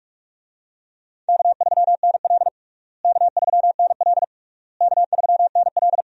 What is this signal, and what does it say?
Morse code, heard as a keyed tone, about 41 wpm, 700 Hz. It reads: K3NL K3NL K3NL
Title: USA Call Signs - 4 Characters - Repeated 3x - 45wpm